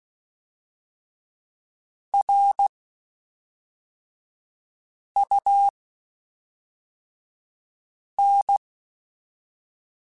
Morse code, heard as a keyed tone, about 16 wpm, spaced as for 3 wpm, 800 Hz. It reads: RUN
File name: (spaced) / QSOs Spaced-mp3s / kc7haa @5 sp